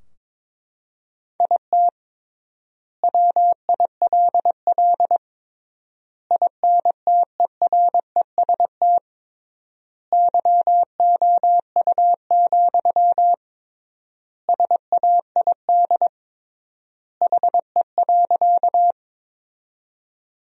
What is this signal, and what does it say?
Morse code, heard as a keyed tone, about 22 wpm, 700 Hz. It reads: IT WILL INTEREST YOU, SAID HE.
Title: 3x Word-Spacing - Sentences from Top 400 Words - 22wpm